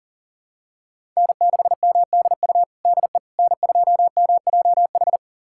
Morse code, heard as a keyed tone, about 40 wpm, 700 Hz. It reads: N6MDU DE N2MJH
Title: USA Callsign DE Callsign - 5 Characters - 40wpm